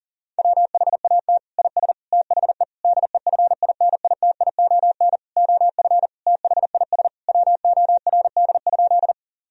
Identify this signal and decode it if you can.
Morse code, heard as a keyed tone, about 37 wpm, 700 Hz. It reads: WHAT IS THE DEFINITION OF THIS WORD?
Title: Sentences from Top 1000 Words - Review Part 4 - 40wpm